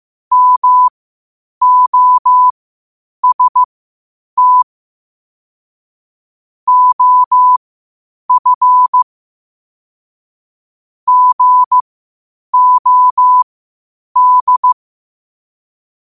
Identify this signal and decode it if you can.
Morse code, heard as a keyed tone, about 15 wpm, spaced as for 8 wpm, 1000 Hz. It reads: MOST OF GOD
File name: qotdfiles / 8wpm